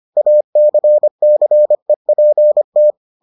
Morse code, heard as a keyed tone, about 25 wpm, 600 Hz. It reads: ACCEPT